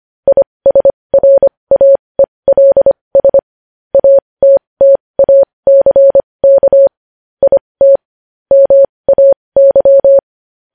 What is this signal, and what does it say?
Morse code, heard as a keyed tone, about 25 wpm, spaced as for 20 wpm, 550 Hz. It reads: ISRAELS ATTACK IT MAY
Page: News Headlines in Morse Code at 20 WPM